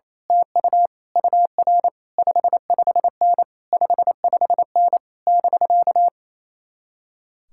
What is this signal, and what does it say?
Morse code, heard as a keyed tone, about 27 wpm, 700 Hz. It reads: TU UR 55N 55N <BK>
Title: Speed-Racing Common QSO Phrases Encore - 28wpm